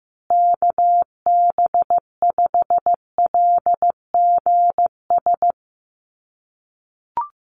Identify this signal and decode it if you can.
Morse code, heard as a keyed tone, about 15 wpm, 700 Hz. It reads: KB5LGS
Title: USA Call Signs Repeated 3x 15wpm